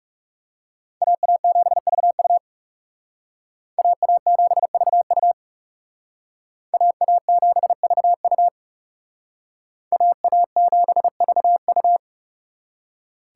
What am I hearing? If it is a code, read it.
Morse code, heard as a keyed tone, about 36 wpm, 700 Hz. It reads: AA7VU AA7VU AA7VU AA7VU
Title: Speed Racing - USA 2x2 Call Signs 30wpm